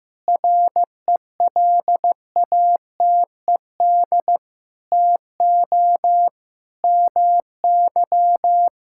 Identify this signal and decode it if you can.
Morse code, heard as a keyed tone, about 15 wpm, 700 Hz. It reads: RELATED TO MY
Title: Sentences from Top 1000 Words - Review Part 5 - 15wpm